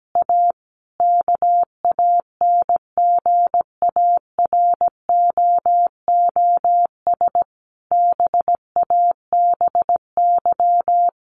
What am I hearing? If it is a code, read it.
Morse code, heard as a keyed tone, about 17 wpm, 700 Hz. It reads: A KANGAROOS BABY
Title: Easy Phrases 17wpm